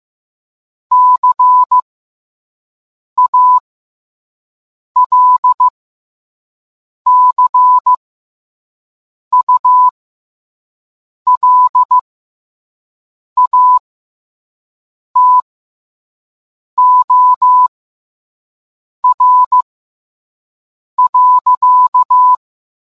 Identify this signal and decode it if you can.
Morse code, heard as a keyed tone, about 15 wpm, spaced as for 5 wpm, 1000 Hz. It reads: CALCULATOR.